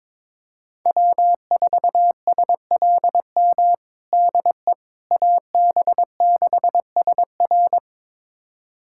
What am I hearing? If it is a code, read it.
Morse code, heard as a keyed tone, about 22 wpm, 700 Hz. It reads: W4SLM DE AB6SR
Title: USA Callsign DE Callsign - 5 Characters - 22wpm